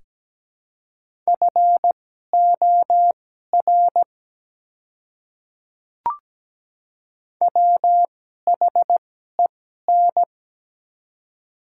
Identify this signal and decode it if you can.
Morse code, heard as a keyed tone, about 17 wpm, spaced as for 12 wpm, 700 Hz. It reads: FOR WHEN
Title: 2x Character Spacing - Top 100 Words - 17wpm